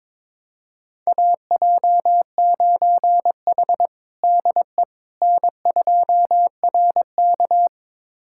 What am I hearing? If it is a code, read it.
Morse code, heard as a keyed tone, about 21 wpm, 700 Hz. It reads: AJ9H DE N2RK
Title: USA Callsign DE Callsign - 4 Characters - 22wpm